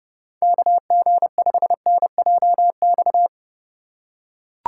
Morse code, heard as a keyed tone, about 30 wpm, 700 Hz. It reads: KG5NJX